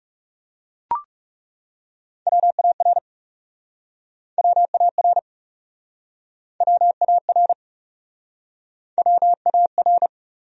Morse code, heard as a keyed tone, about 41 wpm, 700 Hz. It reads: WAR WAR WAR WAR